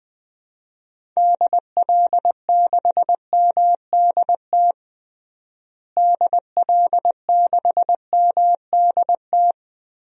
Morse code, heard as a keyed tone, about 20 wpm, 700 Hz. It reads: DL6MDT DL6MDT
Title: DXCC-100 Call Signs - 6 Characters - Repeated 2x - 20wpm